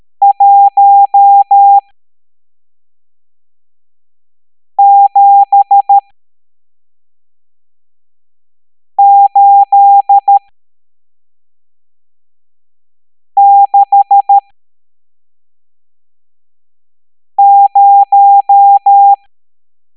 Morse code, heard as a keyed tone, about 13 wpm, spaced as for 3 wpm, 800 Hz. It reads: 17860
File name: hamdata / L12-EX02-CHIFFRES